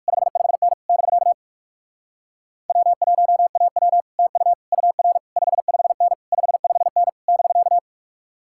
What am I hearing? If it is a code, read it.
Morse code, heard as a keyed tone, about 60 wpm, 700 Hz. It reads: 55N <BK> W1AW TU UR 55N 55N <BK>